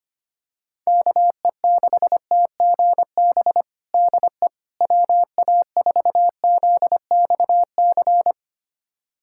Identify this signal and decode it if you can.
Morse code, heard as a keyed tone, about 25 wpm, 700 Hz. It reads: KE6TGB DE WA4ZXC
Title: USA Callsign DE Callsign 25wpm